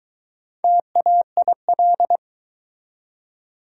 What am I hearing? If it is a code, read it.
Morse code, heard as a keyed tone, about 22 wpm, 700 Hz. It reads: TAIL